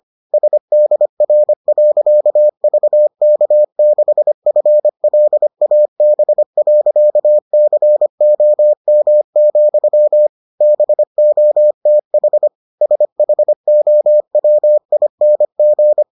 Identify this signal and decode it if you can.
Morse code, heard as a keyed tone, about 25 wpm, 600 Hz. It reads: SDR.VK6FLAB.COM, BOTH SHOWING